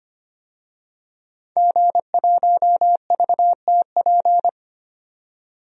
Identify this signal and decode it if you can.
Morse code, heard as a keyed tone, about 25 wpm, 700 Hz. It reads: G1VTP
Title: DXCC-100 Call Signs - 5 Characters - 25wpm